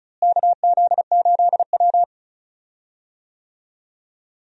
Morse code, heard as a keyed tone, about 36 wpm, 700 Hz. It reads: KZ8W
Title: USA Call Signs - 4 Characters - Repeated 3x - 35wpm